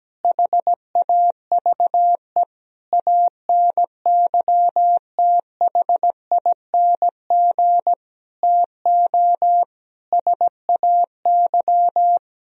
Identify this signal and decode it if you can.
Morse code, heard as a keyed tone, about 17 wpm, 700 Hz. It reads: HAVE ANYTHING TO SAY
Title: Sentences from Top 1000 Words - Review Part 1 - 17wpm